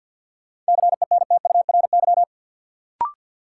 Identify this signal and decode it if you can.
Morse code, heard as a keyed tone, about 50 wpm, 700 Hz. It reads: CENTURY
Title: Sets of 5 Words - New Word Drill - 50wpm